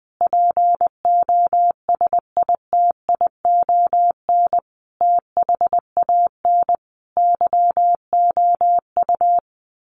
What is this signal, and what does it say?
Morse code, heard as a keyed tone, about 20 wpm, 700 Hz. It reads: POSITION THAN YOU
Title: Sentences from Top 1400 Words 20wpm